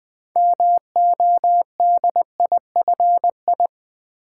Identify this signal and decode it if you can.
Morse code, heard as a keyed tone, about 20 wpm, 700 Hz. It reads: MODIFI
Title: Build A Word - 6 Letters At A Time - 12-Letter Words - 20wpm